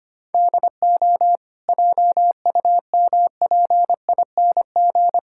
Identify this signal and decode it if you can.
Morse code, heard as a keyed tone, about 25 wpm, 700 Hz. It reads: DO JUMPING
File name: Easy Sentences with Hints - 25wpm